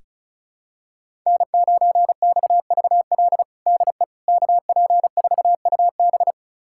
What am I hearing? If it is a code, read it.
Morse code, heard as a keyed tone, about 36 wpm, 700 Hz. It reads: N9XVL DE KP4UB
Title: USA Callsign DE Callsign 35wpm